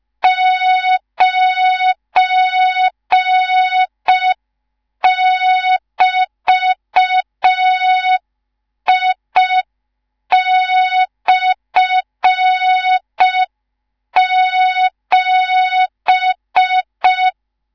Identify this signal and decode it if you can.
Morse code, heard as a keyed tone, about 5 wpm, 3700 Hz. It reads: 9<BT>I/7
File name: Morze-HA2MN / hang / 19 GYAKORLAT